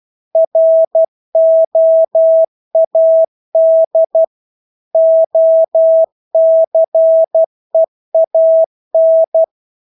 Morse code, heard as a keyed tone, about 12 wpm, 650 Hz. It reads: ROAD OCEAN
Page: Engelske ord 12 wpm